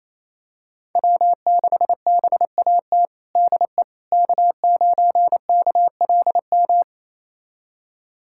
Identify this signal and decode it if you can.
Morse code, heard as a keyed tone, about 28 wpm, 690 Hz. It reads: W6BAT DE K9KLM
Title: USA Callsign DE Callsign - 5 Characters - 28wpm